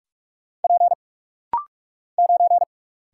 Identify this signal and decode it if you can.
Morse code, heard as a keyed tone, about 44 wpm, 700 Hz. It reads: P 9
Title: Single Letter-Number Alt - Rapid-Fire 45wpm